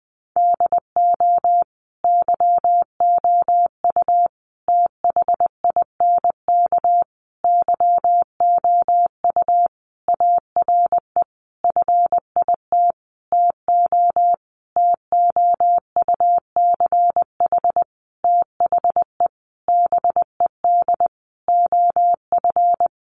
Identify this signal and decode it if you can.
Morse code, heard as a keyed tone, about 20 wpm, 700 Hz. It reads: DO YOU THINK YOU ARE FIT TO TOUCH THE BED OF
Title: Sentences from Top 1200 Words 20wpm